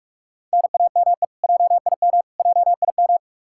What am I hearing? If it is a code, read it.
Morse code, heard as a keyed tone, about 45 wpm, 700 Hz. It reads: NAME JIM JIM
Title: Practice QSO - Exchange 1 - Beginning - 45wpm